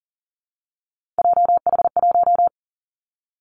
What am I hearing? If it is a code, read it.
Morse code, heard as a keyed tone, about 40 wpm, 700 Hz. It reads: JH1
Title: Callsign Prefixes for Top 25 DXCC - Repeated 3x - 40wpm